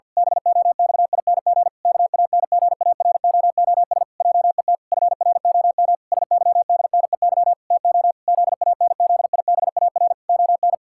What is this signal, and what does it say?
Morse code, heard as a keyed tone, about 48 wpm, 700 Hz. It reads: BOXING KANGAROOS JET FROM SYDNEY TO ZANZIBAR ON
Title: Pangram Challenge v2 - 50wpm